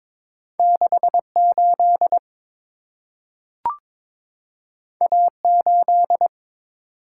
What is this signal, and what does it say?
Morse code, heard as a keyed tone, about 22 wpm, 700 Hz. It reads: 68 A8
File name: practice / 22wpm / Two Letter-Number Combinations - 22wpm